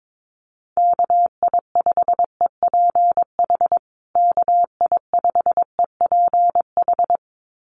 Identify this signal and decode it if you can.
Morse code, heard as a keyed tone, about 22 wpm, 700 Hz. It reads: KI5EPH KI5EPH
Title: USA Call Signs Repeated 2x 22wpm